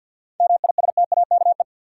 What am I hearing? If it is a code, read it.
Morse code, heard as a keyed tone, about 51 wpm, 700 Hz. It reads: MISTAKE
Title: Seven Letter Words - 50wpm